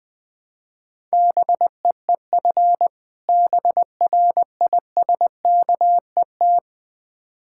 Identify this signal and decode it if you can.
Morse code, heard as a keyed tone, about 20 wpm, 700 Hz. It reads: BEEF BRISKET
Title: Difficult Words with Hints - 20wpm